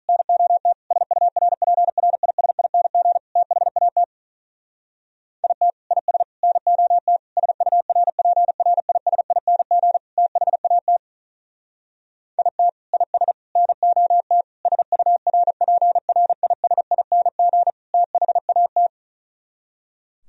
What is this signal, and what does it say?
Morse code, heard as a keyed tone, about 44 wpm, 700 Hz. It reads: NOT SURPRISING THAT IT IS NOT SURPRISING THAT IT IS NOT SURPRISING THAT
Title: Speed-Racing - Sets of 5 Words 35wpm